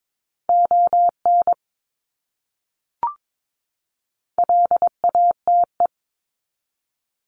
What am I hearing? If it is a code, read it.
Morse code, heard as a keyed tone, about 22 wpm, 700 Hz. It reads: ON LATE
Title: Top 300 Words 22wpm